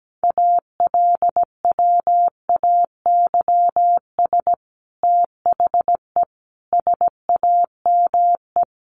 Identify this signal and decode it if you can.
Morse code, heard as a keyed tone, about 17 wpm, 700 Hz. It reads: ALWAYS THE SAME
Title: Sentences from Top 300 Words 17wpm